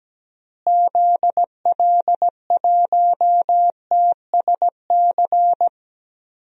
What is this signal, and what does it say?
Morse code, heard as a keyed tone, about 17 wpm, 700 Hz. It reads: ZL1TSC
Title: DXCC-100 Call Signs - 6 Characters - Repeated 3x - 17wpm